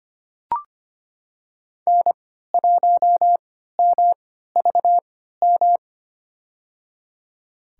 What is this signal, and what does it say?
Morse code, heard as a keyed tone, about 25 wpm, spaced as for 14 wpm, 700 Hz. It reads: N1MVM
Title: USA Call Signs - 5 Characters - 3x Character Spacing - 25wpm